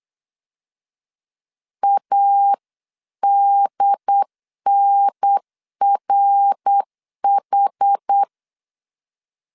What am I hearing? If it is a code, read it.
Morse code, heard as a keyed tone, about 8 wpm, 800 Hz. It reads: ADNRH
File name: CW / DET / 1 3 1500C S C 8wpm 4214